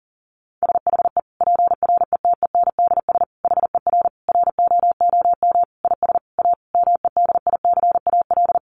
Morse code, heard as a keyed tone, about 40 wpm, 700 Hz. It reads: SHE PRETENDS HER ROOM IS A MEDICAL
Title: Additional Easy Phrases 40wpm